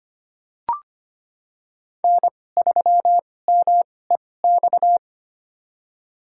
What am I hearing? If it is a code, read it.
Morse code, heard as a keyed tone, about 25 wpm, spaced as for 18 wpm, 700 Hz. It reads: N3MEX
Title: USA Call Signs - 5 Characters - 2x Character Spacing - 25wpm